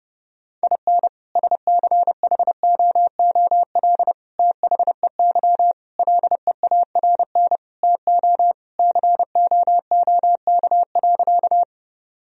Morse code, heard as a keyed tone, about 30 wpm, 700 Hz. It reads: IN SCHOOL THEY LEARN TO COOK.
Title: Sentences from Top 1000 Words - Review Part 4 - 30wpm